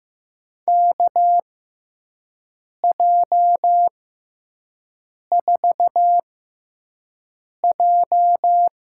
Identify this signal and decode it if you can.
Morse code, heard as a keyed tone, about 16 wpm, spaced as for 5 wpm, 700 Hz. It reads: KJ4J
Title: USA Call Signs - 6x Character Spacing - 15wpm